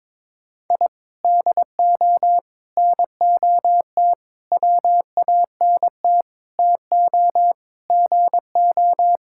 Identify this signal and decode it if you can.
Morse code, heard as a keyed tone, about 22 wpm, 700 Hz. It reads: I DO NOT WANT TO GO
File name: practice / 22wpm / Sentences with 4-Word Phrases - 22wpm